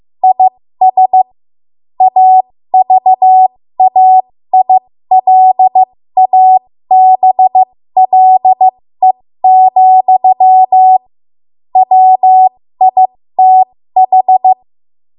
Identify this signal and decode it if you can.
Morse code, heard as a keyed tone, about 15 wpm, spaced as for 13 wpm, 750 Hz. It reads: IS AVAILABLE, WITH
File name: Morse / 13WPM